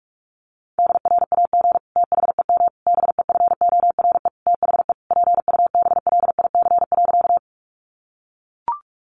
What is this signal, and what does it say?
Morse code, heard as a keyed tone, about 44 wpm, 700 Hz. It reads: DRAG THEM BEFORE THE PUBLIC.
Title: Sentences from Top 1500 Words 45wpm